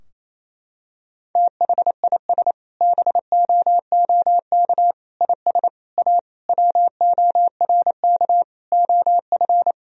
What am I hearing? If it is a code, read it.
Morse code, heard as a keyed tone, about 28 wpm, 700 Hz. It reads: THIS BOOK IS A WORK OF
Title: Sentences from Top 1000 Words - Review Part 3 - 28wpm